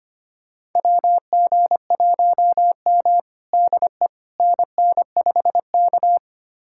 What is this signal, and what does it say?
Morse code, heard as a keyed tone, about 25 wpm, 700 Hz. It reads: WG1M DE NN5K
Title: USA Callsign DE Callsign - 4 Characters - 25wpm